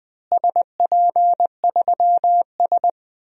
Morse code, heard as a keyed tone, about 20 wpm, 700 Hz. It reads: SP3S
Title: DXCC-100 Call Signs - 4 Characters - Repeated 2x - 20wpm